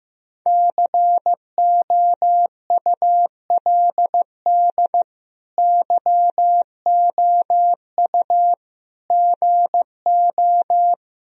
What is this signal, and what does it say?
Morse code, heard as a keyed tone, about 15 wpm, 700 Hz. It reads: COULD YOU GO
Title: Sentences from Top 1000 Words - Review Part 4 - 15wpm